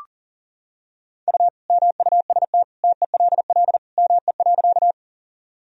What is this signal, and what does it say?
Morse code, heard as a keyed tone, about 37 wpm, 700 Hz. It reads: U MUST TELL ME.
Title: Sentences from Top 200 Words and Abbreviations 40wpm